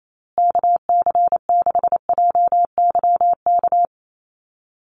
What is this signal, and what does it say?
Morse code, heard as a keyed tone, about 28 wpm, 700 Hz. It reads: KC6JYK
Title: USA Call Signs Repeated 3x 28wpm